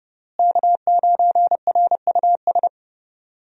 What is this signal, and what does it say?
Morse code, heard as a keyed tone, about 32 wpm, 700 Hz. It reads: K9RUS
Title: USA Call Signs - 5 Characters - Repeated 3x - 30wpm